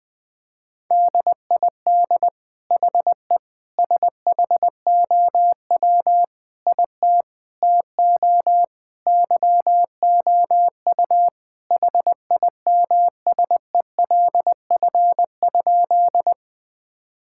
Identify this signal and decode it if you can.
Morse code, heard as a keyed tone, about 20 wpm, 700 Hz. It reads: DID HE SHOW IT TO YOU HIMSELF?
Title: Sentences from Top 1600 Words 20wpm